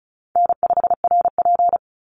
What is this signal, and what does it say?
Morse code, heard as a keyed tone, about 34 wpm, 700 Hz. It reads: N5RP
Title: Call Signs - 35wpm